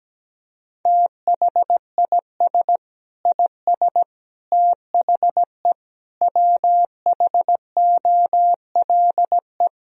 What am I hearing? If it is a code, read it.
Morse code, heard as a keyed tone, about 17 wpm, 700 Hz. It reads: THIS IS THE WHOLE
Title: Sentences from Top 1000 Words - Review Part 1 - 17wpm